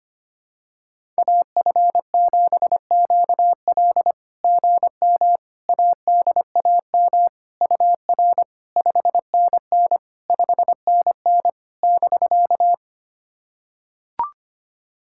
Morse code, heard as a keyed tone, about 25 wpm, 690 Hz. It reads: AF7QL GM ADAM UR 5NN 5NN <BK>
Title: SOTA QSO Practice 25wpm